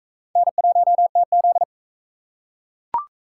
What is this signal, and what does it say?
Morse code, heard as a keyed tone, about 42 wpm, 700 Hz. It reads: N1TZ